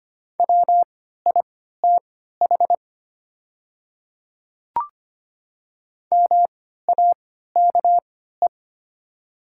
Morse code, heard as a keyed tone, about 25 wpm, spaced as for 14 wpm, 700 Hz. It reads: WITH MAKE
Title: 3x Character Spacing - Top 100 Words - 25wpm